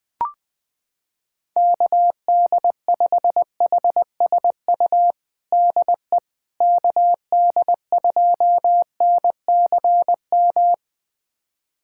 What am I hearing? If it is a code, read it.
Morse code, heard as a keyed tone, about 20 wpm, 700 Hz. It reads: KD5HSU DE KD2NCM
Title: USA Callsign DE Callsign - 6 Characters - 20wpm